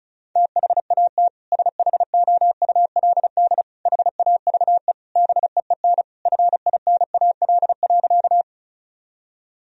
Morse code, heard as a keyed tone, about 34 wpm, 700 Hz. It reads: THAT SHOULD HAVE BEEN FINAL.
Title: Sentences from Top 1000 Words - Review Part 2 - 35wpm